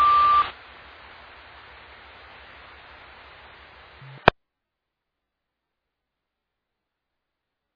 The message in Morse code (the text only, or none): none